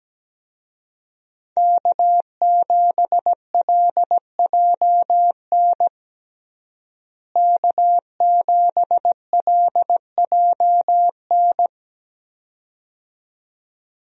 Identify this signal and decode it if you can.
Morse code, heard as a keyed tone, about 17 wpm, 700 Hz. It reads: K7LJN K7LJN
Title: USA Call Signs - 5 Characters - Repeated 2x - 17wpm